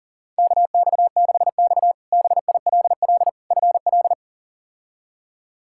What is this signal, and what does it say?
Morse code, heard as a keyed tone, about 40 wpm, 690 Hz. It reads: KX6X BILL FL
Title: K1USN SST QSO Practice - 40wpm